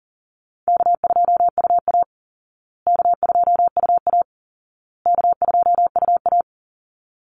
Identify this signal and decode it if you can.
Morse code, heard as a keyed tone, about 40 wpm, 700 Hz. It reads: K2UA K2UA K2UA
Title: Easy Call Signs Repeated 3x - 40wpm